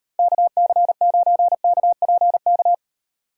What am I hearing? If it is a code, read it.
Morse code, heard as a keyed tone, about 38 wpm, 700 Hz. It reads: KC9KPK